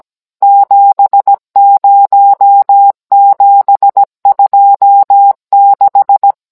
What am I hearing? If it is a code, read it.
Morse code, heard as a keyed tone, about 17 wpm, 800 Hz. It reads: 70726